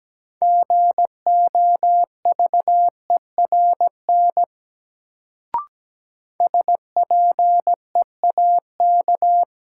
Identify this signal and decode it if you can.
Morse code, heard as a keyed tone, about 17 wpm, 700 Hz. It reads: GOVERN SPEAK
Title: Sentences from Top 1000 Words - Review Part 2 - New Word Drill - 17wpm